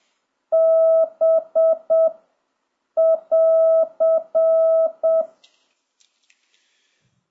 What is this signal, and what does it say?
Morse code, heard as a keyed tone, about 7 wpm, 1300 Hz. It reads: B <AR>
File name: jh8cbh / 2026cw